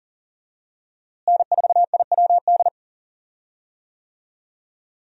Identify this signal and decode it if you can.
Morse code, heard as a keyed tone, about 40 wpm, 690 Hz. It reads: N4IWD
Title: USA Call Signs - 5 Characters - Repeated 3x - 40wpm